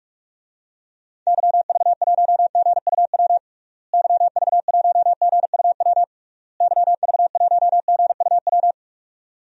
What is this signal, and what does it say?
Morse code, heard as a keyed tone, about 41 wpm, 700 Hz. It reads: YV1GUW YV1GUW YV1GUW
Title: DXCC-100 Call Signs - 6 Characters - Repeated 3x - 45wpm